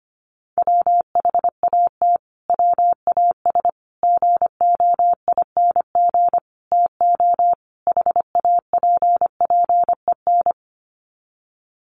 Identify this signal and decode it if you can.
Morse code, heard as a keyed tone, about 25 wpm, 700 Hz. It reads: WHAT WAS GOING TO HAPPEN
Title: Sets of 5 Words Encore 25wpm